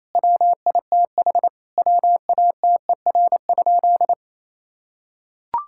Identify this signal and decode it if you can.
Morse code, heard as a keyed tone, about 28 wpm, 700 Hz. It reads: WITH WATER?
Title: Sentences from Top 1000 Words - Review Part 5 - 28wpm